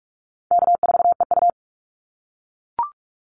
Morse code, heard as a keyed tone, about 44 wpm, 700 Hz. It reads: K4EU
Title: Easy Call Signs - 45wpm